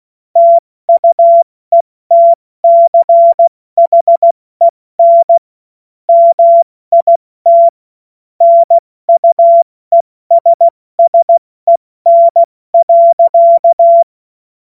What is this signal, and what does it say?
Morse code, heard as a keyed tone, about 16 wpm, 680 Hz. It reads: TUETCHEN MIT NUESSEN.
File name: Morsekurs / Witze / Witz 05. Oktober 2021, 70 F80